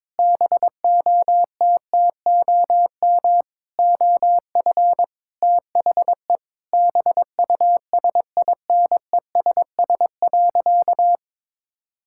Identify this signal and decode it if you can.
Morse code, heard as a keyed tone, about 22 wpm, 700 Hz. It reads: BOTTOM OF THE BUSINESS.
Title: Sentences from Top 1000 Words - Review Part 6 - 22wpm